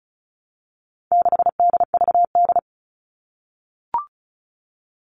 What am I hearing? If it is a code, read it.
Morse code, heard as a keyed tone, about 34 wpm, 700 Hz. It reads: 6DVD
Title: All Cut Numbers - 4-Digit 35wpm